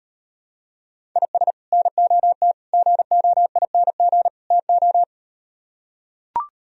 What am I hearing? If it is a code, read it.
Morse code, heard as a keyed tone, about 38 wpm, 700 Hz. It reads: IS NOT GOING TO